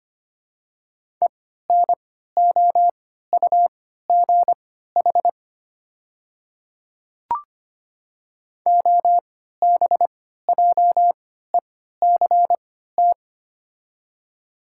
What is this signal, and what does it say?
Morse code, heard as a keyed tone, about 25 wpm, spaced as for 14 wpm, 700 Hz. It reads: ENOUGH OBJECT
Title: 3x Character Spacing - Top 500 Words - 25wpm